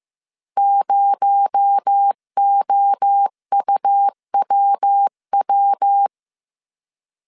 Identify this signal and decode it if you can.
Morse code, heard as a keyed tone, about 15 wpm, 800 Hz. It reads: 0OUWW